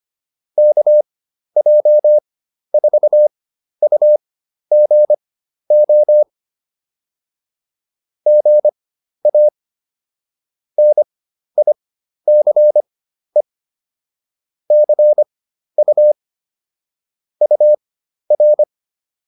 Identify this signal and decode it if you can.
Morse code, heard as a keyed tone, about 25 wpm, spaced as for 12 wpm, 600 Hz. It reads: KJ4UGO GA NICE CU UR